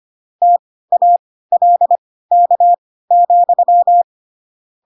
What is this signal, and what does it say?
Morse code, heard as a keyed tone, about 25 wpm, spaced as for 16 wpm, 700 Hz. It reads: TALK,
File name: Pinocchio / Pinocchio ch7 10 25 ik1hge